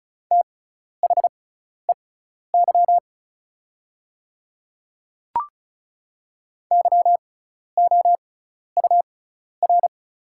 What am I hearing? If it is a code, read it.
Morse code, heard as a keyed tone, about 36 wpm, spaced as for 12 wpm, 700 Hz. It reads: THEY YOUR
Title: 6x Character Spacing - Top 200 Words - 35wpm